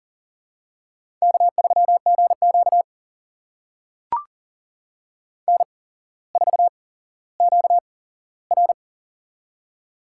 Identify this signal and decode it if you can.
Morse code, heard as a keyed tone, about 39 wpm, spaced as for 17 wpm, 700 Hz. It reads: K3GQ N4QR
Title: USA Call Signs - 4 Characters - 8x Character Spacing - 40wpm